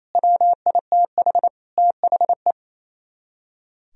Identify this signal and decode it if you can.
Morse code, heard as a keyed tone, about 34 wpm, 700 Hz. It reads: WITH THE
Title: Speed-Racing - Sets of 4 Words 28wpm